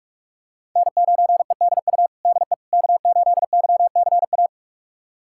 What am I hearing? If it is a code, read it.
Morse code, heard as a keyed tone, about 42 wpm, 700 Hz. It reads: N9EDU DE K8YCA
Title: USA Callsign DE Callsign - 5 Characters - 45wpm